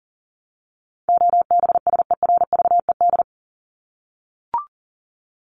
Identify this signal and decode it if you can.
Morse code, heard as a keyed tone, about 40 wpm, 700 Hz. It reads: OBSERVED
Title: ED Suffix - Top 100 Words with Base Words 40wpm